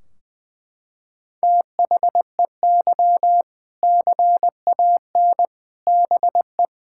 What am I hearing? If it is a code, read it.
Morse code, heard as a keyed tone, about 21 wpm, 700 Hz. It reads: THEY CAN BE
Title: Sentences from Top 1600 Words 20wpm